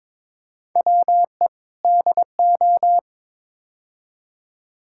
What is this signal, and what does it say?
Morse code, heard as a keyed tone, about 22 wpm, 700 Hz. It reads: WE DO
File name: practice / 22wpm / Build A Phrase - 5-Word Phrases - 22wpm